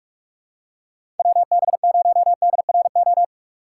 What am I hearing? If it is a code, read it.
Morse code, heard as a keyed tone, about 41 wpm, 700 Hz. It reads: WB0DRO
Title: USA Call Signs - 2x Character Spacing - 45wpm